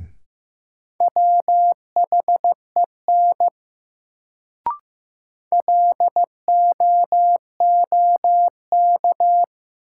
Morse code, heard as a keyed tone, about 15 wpm, 700 Hz. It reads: WHEN LOOK
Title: Sets of 2 Words - New Word Drill - 15wpm